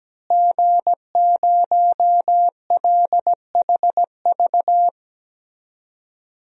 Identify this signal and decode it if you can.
Morse code, heard as a keyed tone, about 17 wpm, 700 Hz. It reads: G0LHV
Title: DXCC-100 Call Signs - 5 Characters - Repeated 3x - 17wpm